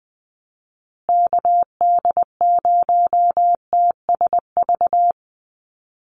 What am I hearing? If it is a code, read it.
Morse code, heard as a keyed tone, about 20 wpm, 700 Hz. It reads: KD0TSV
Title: USA Call Signs Repeated 2x 20wpm